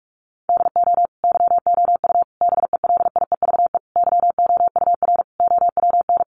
Morse code, heard as a keyed tone, about 44 wpm, 700 Hz. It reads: DO YOU BELIEVE YOUR OWN
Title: Sentences from Top 1000 Words 45wpm